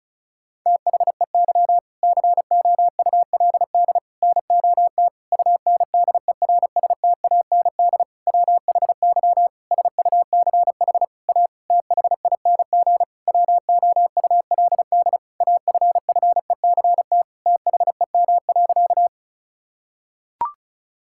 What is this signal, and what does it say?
Morse code, heard as a keyed tone, about 36 wpm, 700 Hz. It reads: THEY COULD NOT UNDERSTAND WHY SUCH A THING WOULD AFFECT THEM.
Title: Sentences from Top 1000 Words - Review Part 6 - 35wpm